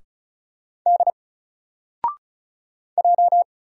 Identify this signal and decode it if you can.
Morse code, heard as a keyed tone, about 36 wpm, 700 Hz. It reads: D J
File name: Single-Letter-Number-35wpm